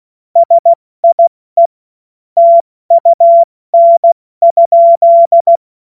Morse code, heard as a keyed tone, about 16 wpm, 680 Hz. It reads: SIE TUN?
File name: Morsekurs / Witze / Witz 10. Mai 2022 70 F80